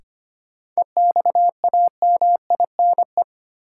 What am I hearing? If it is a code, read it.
Morse code, heard as a keyed tone, about 25 wpm, 700 Hz. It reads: EXAMINE
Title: Sentences from Top 1600 Words - New Word Drill Alt - 25wpm